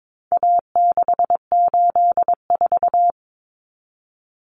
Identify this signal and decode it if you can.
Morse code, heard as a keyed tone, about 22 wpm, 700 Hz. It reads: A684
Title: Common Cut Numbers - 4-Digit 22wpm